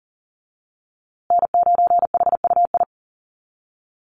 Morse code, heard as a keyed tone, about 40 wpm, 700 Hz. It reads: N9HUI
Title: USA Call Signs Repeated 3x 40wpm